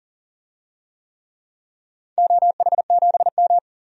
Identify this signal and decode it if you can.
Morse code, heard as a keyed tone, about 40 wpm, 700 Hz. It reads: OH7M
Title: DXCC-100 Call Signs - 4 Characters - 40wpm